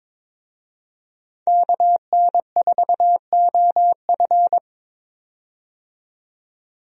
Morse code, heard as a keyed tone, about 22 wpm, 700 Hz. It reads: KN4OF
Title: USA Call Signs - 5 Characters - 22wpm